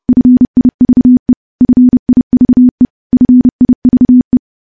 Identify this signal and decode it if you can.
Morse code, heard as a keyed tone, about 30 wpm, 250 Hz. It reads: FIVE FIVE FIVE
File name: morse / morse-training-30WPM